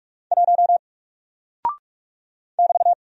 Morse code, heard as a keyed tone, about 44 wpm, 700 Hz. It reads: 1 <BT>
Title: Single Character (Letter:Number:Punctuation:Prosign) 45wpm